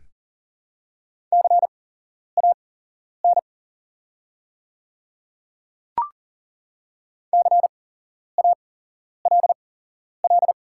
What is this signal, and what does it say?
Morse code, heard as a keyed tone, about 39 wpm, spaced as for 11 wpm, 700 Hz. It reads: CAN CALL
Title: 8x Character Spacing - Top 300 Words - 40wpm